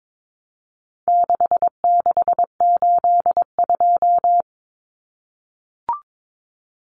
Morse code, heard as a keyed tone, about 22 wpm, 700 Hz. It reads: 6682
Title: Numbers - 4-Digit 22wpm